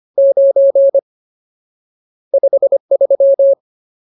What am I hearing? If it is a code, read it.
Morse code, heard as a keyed tone, about 25 wpm, 550 Hz. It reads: 9 53
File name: Ultimate Word List / 25WPM Ultimate